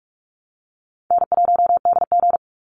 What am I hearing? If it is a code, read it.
Morse code, heard as a keyed tone, about 44 wpm, 700 Hz. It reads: N1DG
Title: Call Signs Repeated 2x - 45wpm